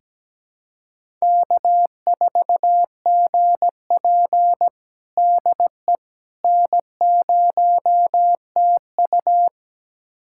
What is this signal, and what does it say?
Morse code, heard as a keyed tone, about 17 wpm, 700 Hz. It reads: K4GP DE N0TU
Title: USA Callsign DE Callsign - 4 Characters - 17wpm